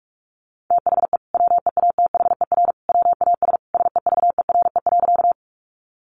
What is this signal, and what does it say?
Morse code, heard as a keyed tone, about 46 wpm, 700 Hz. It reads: THE WEATHER WAS SEVERE.
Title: Sentences from Top 1400 Words 45wpm